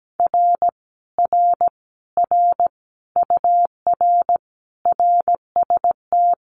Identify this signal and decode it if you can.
Morse code, heard as a keyed tone, about 17 wpm, 700 Hz. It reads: R R R UR RST
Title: Common QSO Phrases Encore 17wpm